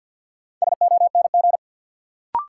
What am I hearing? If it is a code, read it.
Morse code, heard as a keyed tone, about 52 wpm, 700 Hz. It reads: SONG
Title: Sentences from Top 400 Words - New Word Drill - 50wpm